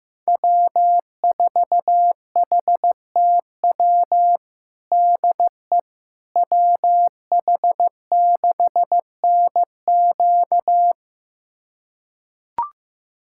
Morse code, heard as a keyed tone, about 15 wpm, 700 Hz. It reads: W4HTW DE WH6NQ
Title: USA Callsign DE Callsign - 5 Characters - 15wpm